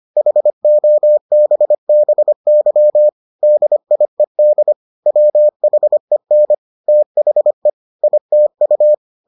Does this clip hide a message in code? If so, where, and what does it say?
Yes: Morse code from 0.2 to 8.9 s, heard as a keyed tone, about 25 wpm, 600 Hz. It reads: HOBBY DIED WHEN THE ITU